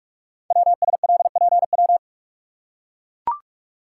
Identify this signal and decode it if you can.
Morse code, heard as a keyed tone, about 44 wpm, 700 Hz. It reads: WSLPW
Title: Code Groups - 5 Letters - 45wpm